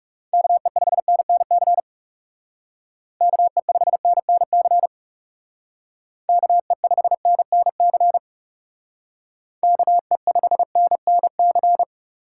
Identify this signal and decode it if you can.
Morse code, heard as a keyed tone, about 40 wpm, 700 Hz. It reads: KE5NNC KE5NNC KE5NNC KE5NNC